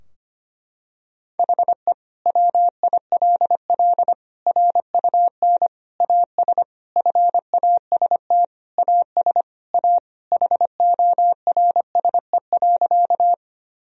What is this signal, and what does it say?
Morse code, heard as a keyed tone, about 25 wpm, 700 Hz. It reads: HE WILL RUN AS FAST AS A HORSE.
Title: Sentences from Top 1000 Words - Review Part 2 - 25wpm